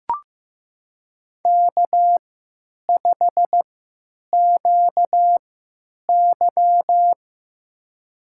Morse code, heard as a keyed tone, about 15 wpm, spaced as for 9 wpm, 700 Hz. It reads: K5QY
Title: USA Call Signs - 4 Characters - 3x Character Spacing - 15wpm